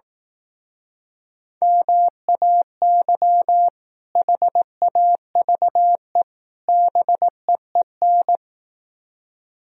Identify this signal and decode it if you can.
Morse code, heard as a keyed tone, about 18 wpm, 700 Hz. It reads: MAY HAVE BEEN
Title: Speed-Racing - Sets of 3 Words 15wpm